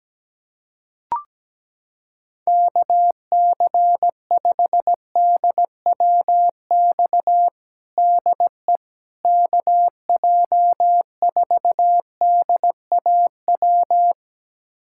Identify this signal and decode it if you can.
Morse code, heard as a keyed tone, about 17 wpm, 700 Hz. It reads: KC5DWX DE KJ4DAW
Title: USA Callsign DE Callsign - 6 Characters - 17wpm